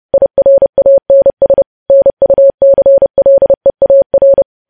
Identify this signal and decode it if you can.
Morse code, heard as a keyed tone, about 30 wpm, 550 Hz. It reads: IRANS NUCLEAR
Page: news headlines in morse code at 30 wpm